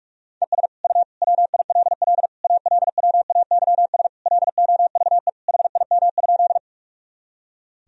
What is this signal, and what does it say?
Morse code, heard as a keyed tone, about 41 wpm, 700 Hz. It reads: ES U WILL ALWAYS LOVE HIM?
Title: Sentences from Top 400 Words and Abbreviations 45wpm